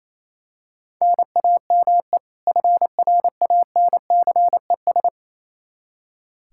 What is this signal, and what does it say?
Morse code, heard as a keyed tone, about 28 wpm, 700 Hz. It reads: NAME FRANCES
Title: Practice QSO - Exchange 1 - Beginning - 28wpm